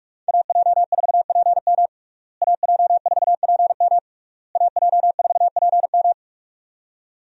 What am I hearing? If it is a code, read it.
Morse code, heard as a keyed tone, about 46 wpm, 700 Hz. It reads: AJ4PM AJ4PM AJ4PM
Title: USA Call Signs - 5 Characters - Repeated 3x - 45wpm